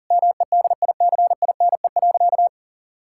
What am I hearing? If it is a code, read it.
Morse code, heard as a keyed tone, about 40 wpm, 700 Hz. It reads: MEDICINE.
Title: Sentences from Top 1000 Words - Review Part 6 - 40wpm